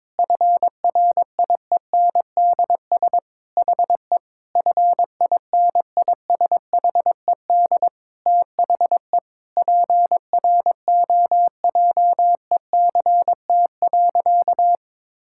Morse code, heard as a keyed tone, about 22 wpm, 700 Hz. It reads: FRIENDS HE FINISHED THE PROJECT.
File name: practice / 22wpm / Sentences with 4-Word Phrases - 22wpm